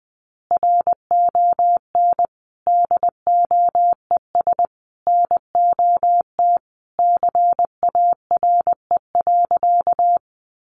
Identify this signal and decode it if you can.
Morse code, heard as a keyed tone, about 20 wpm, 700 Hz. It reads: RON DOES NOT CARE.
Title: Additional Easy Phrases 20wpm